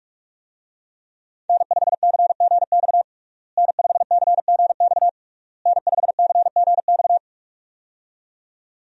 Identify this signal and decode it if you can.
Morse code, heard as a keyed tone, about 46 wpm, 700 Hz. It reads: N5CGX N5CGX N5CGX
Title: USA Call Signs - 5 Characters - Repeated 3x - 45wpm